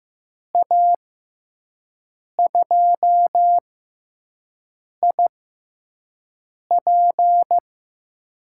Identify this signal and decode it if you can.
Morse code, heard as a keyed tone, about 15 wpm, spaced as for 5 wpm, 700 Hz. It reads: A2IP
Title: USA Call Signs - 5 Characters - 6x Character Spacing - 15wpm